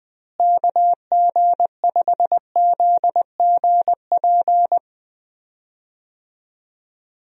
Morse code, heard as a keyed tone, about 20 wpm, 700 Hz. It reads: KG5ZGP
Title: USA Call Signs - 6 Characters - Repeated 3x - 20wpm